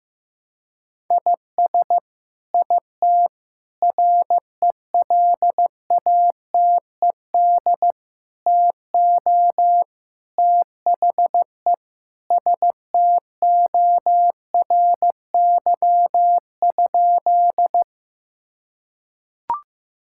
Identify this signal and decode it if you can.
Morse code, heard as a keyed tone, about 15 wpm, 700 Hz. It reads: IS IT RELATED TO THE STORY?
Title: Sentences from Top 1000 Words - Review Part 5 - 15wpm